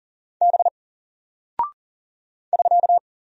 Morse code, heard as a keyed tone, about 40 wpm, 700 Hz. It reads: B <SK>
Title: Single Character (Letter:Number:Punctuation:Prosign) Alt 40wpm